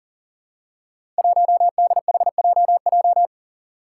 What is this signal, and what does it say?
Morse code, heard as a keyed tone, about 39 wpm, 700 Hz. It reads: 1DHJJ
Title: Five Challenging Letter-Number Combinations 40wpm